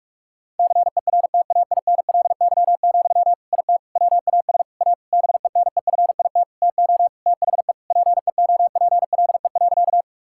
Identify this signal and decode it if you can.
Morse code, heard as a keyed tone, about 46 wpm, 700 Hz. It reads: CERTAINLY, IT WAS A BENEFIT TO THE PEOPLE.
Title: Sentences from Top 1000 Words - Review Part 6 - 45wpm